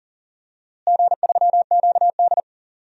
Morse code, heard as a keyed tone, about 40 wpm, 700 Hz. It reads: G3QD
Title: DXCC-100 Call Signs - 4 Characters - Repeated 3x - 40wpm